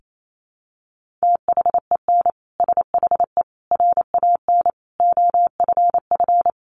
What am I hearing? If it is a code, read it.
Morse code, heard as a keyed tone, about 28 wpm, 700 Hz. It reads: THEN SHE RAN OFF
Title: Sentences from Top 1400 Words 28wpm